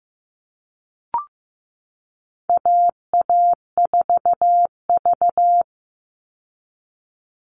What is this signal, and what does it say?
Morse code, heard as a keyed tone, about 15 wpm, 700 Hz. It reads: AA4V
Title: Call Signs Repeated 2x - 15wpm